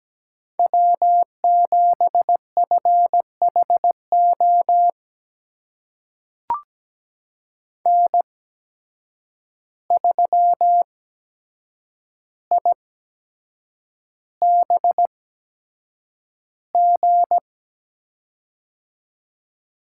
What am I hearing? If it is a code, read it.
Morse code, heard as a keyed tone, about 17 wpm, spaced as for 7 wpm, 700 Hz. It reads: W7FHO N3IBG
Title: USA Call Signs - 5 Characters - 8x Character Spacing - 17wpm